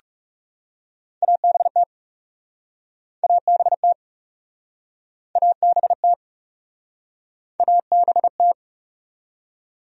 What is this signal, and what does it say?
Morse code, heard as a keyed tone, about 38 wpm, 700 Hz. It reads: ABT ABT ABT ABT